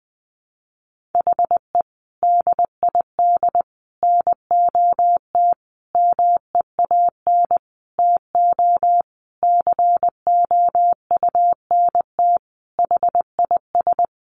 Morse code, heard as a keyed tone, about 20 wpm, 700 Hz. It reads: HE DID NOT MEAN TO COUNT HIS
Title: Sentences from Top 1500 Words 20wpm